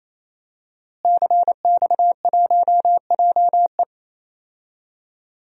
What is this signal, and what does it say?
Morse code, heard as a keyed tone, about 28 wpm, 700 Hz. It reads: CX1JE
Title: DXCC-100 Call Signs - 5 Characters - Repeated 3x - 28wpm